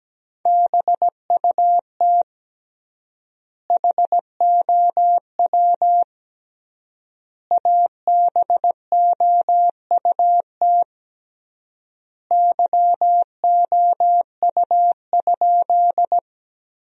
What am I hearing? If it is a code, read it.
Morse code, heard as a keyed tone, about 17 wpm, 700 Hz. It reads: BUT HOW ABOUT YOU?
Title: 3x Word-Spacing - Sentences from Top 100 Words - 17wpm